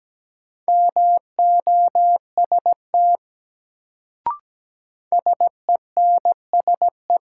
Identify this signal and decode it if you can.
Morse code, heard as a keyed tone, about 17 wpm, 700 Hz. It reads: MOST SENSE
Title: Sets of 2 Words - New Word Drill - 17wpm